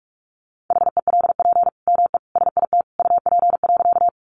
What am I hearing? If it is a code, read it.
Morse code, heard as a keyed tone, about 44 wpm, 700 Hz. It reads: HELP ME SIT UP.
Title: Sentences from Top 500 Words 45wpm